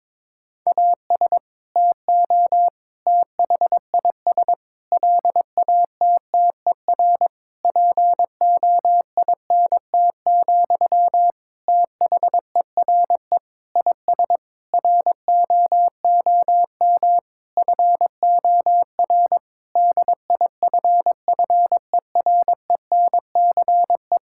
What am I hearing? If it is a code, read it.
Morse code, heard as a keyed tone, about 22 wpm, 700 Hz. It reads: AS TO THIS LATTER POINT, THERE IS ROOM FOR DIFFERENCE
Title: Sentences from Top 1600 Words 22wpm